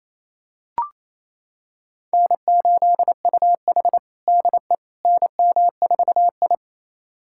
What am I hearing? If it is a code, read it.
Morse code, heard as a keyed tone, about 28 wpm, 700 Hz. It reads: N8UH DE NM4I
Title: USA Callsign DE Callsign - 4 Characters - 28wpm